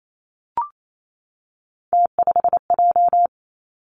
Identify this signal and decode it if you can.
Morse code, heard as a keyed tone, about 28 wpm, 700 Hz. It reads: T5J
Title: Three Challenging Letter-Number Combinations 28wpm